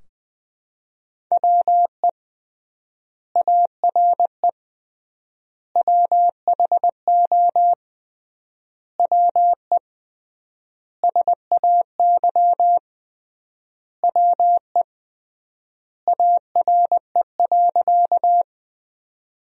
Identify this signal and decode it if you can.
Morse code, heard as a keyed tone, about 20 wpm, 700 Hz. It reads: WE ARE WHO WE SAY WE ARE.
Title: 3x Word-Spacing - Sentences from Top 200 Words - 20wpm